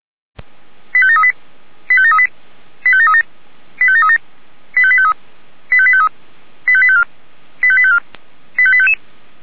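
Five-tone selective call (ZVEI), heard as quick runs of five tones, tones of about 70 ms. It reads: 85738 85738 85737 85737 85853 85853 85854 85854 85855
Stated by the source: ZVEI five-tone selective call received by SDR.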